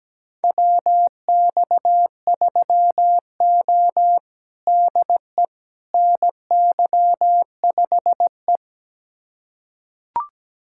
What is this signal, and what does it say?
Morse code, heard as a keyed tone, about 17 wpm, 700 Hz. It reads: WX3O DE NY5E
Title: USA Callsign DE Callsign - 4 Characters - 17wpm